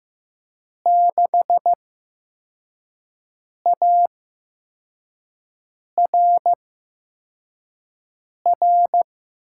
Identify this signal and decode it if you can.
Morse code, heard as a keyed tone, about 15 wpm, spaced as for 4 wpm, 700 Hz. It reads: 6ARR
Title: USA Call Signs - 6 Characters - 8x Character Spacing - 15wpm